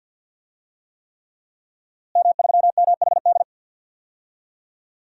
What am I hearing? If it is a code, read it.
Morse code, heard as a keyed tone, about 53 wpm, 700 Hz. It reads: M3MHD
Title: DXCC-100 Call Signs - 5 Characters - Repeated 3x - 50wpm